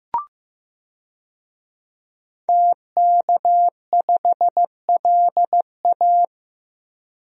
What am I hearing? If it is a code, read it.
Morse code, heard as a keyed tone, about 15 wpm, 700 Hz. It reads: TK5LA
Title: DXCC-100 Call Signs - 5 Characters - 15wpm